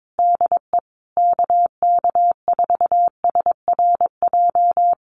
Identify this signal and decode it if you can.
Morse code, heard as a keyed tone, about 22 wpm, 700 Hz. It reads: DE KK4SRJ
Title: Common QSO Phrases Encore 22wpm